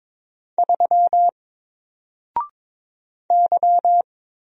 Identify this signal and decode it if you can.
Morse code, heard as a keyed tone, about 22 wpm, 700 Hz. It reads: 3 Y
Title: Single Letter-Number Alt 22wpm